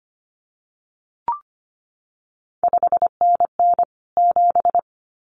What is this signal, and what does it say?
Morse code, heard as a keyed tone, about 25 wpm, 700 Hz. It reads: 5NN 7
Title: CQ WW DX Contest Exchange - 25wpm